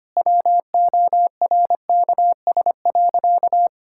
Morse code, heard as a keyed tone, about 25 wpm, 700 Hz. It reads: WORKS.
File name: practice / 25wpm / Sentences with 3-Word Phrases - 25wpm